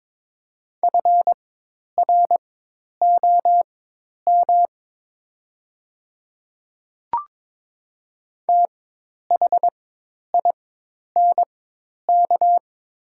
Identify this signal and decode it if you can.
Morse code, heard as a keyed tone, about 22 wpm, spaced as for 10 wpm, 700 Hz. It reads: FROM THINK
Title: 4x Character Spacing - Top 200 Words - 22wpm